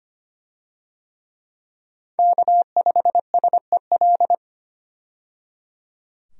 Morse code, heard as a keyed tone, about 25 wpm, 700 Hz. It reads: K5SEL
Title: USA Call Signs - 5 Characters - 25wpm